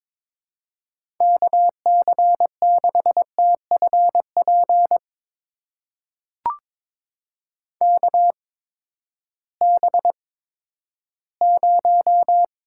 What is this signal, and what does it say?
Morse code, heard as a keyed tone, about 21 wpm, spaced as for 12 wpm, 700 Hz. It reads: KC6TFP KB0
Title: USA Call Signs - 6 Characters - 8x Character Spacing - 22wpm